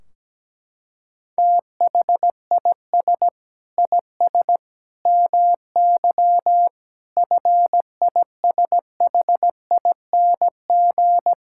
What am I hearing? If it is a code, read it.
Morse code, heard as a keyed tone, about 17 wpm, 700 Hz. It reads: THIS IS MY FISHING
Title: Sentences from Top 1600 Words 17wpm